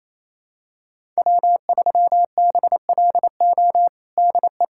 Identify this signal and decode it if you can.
Morse code, heard as a keyed tone, about 28 wpm, 700 Hz. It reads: W3BLO DE
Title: USA Callsign DE Callsign - 5 Characters - 28wpm